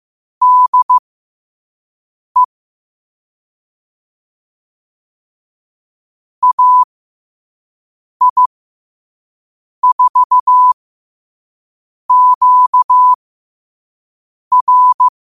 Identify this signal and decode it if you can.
Morse code, heard as a keyed tone, about 15 wpm, spaced as for 5 wpm, 1000 Hz. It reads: DE AI4QR
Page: New quotes every day in morse code at 5 Words per minute.